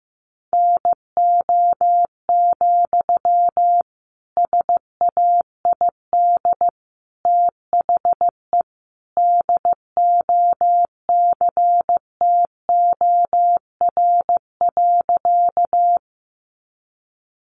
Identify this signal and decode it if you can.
Morse code, heard as a keyed tone, about 15 wpm, 700 Hz. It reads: NO, SAID THE DOCTOR.
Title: Sentences from Top 1500 Words 15wpm